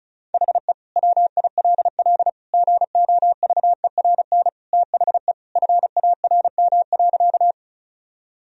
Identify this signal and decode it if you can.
Morse code, heard as a keyed tone, about 36 wpm, 700 Hz. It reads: HE WILL GOVERN THE FARM.
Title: Sentences from Top 1000 Words - Review Part 2 - 35wpm